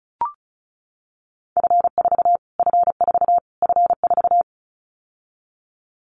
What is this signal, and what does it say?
Morse code, heard as a keyed tone, about 34 wpm, 700 Hz. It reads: F4 F4 F4
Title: Callsign Prefixes for Top 25 DXCC - Repeated 3x - 35wpm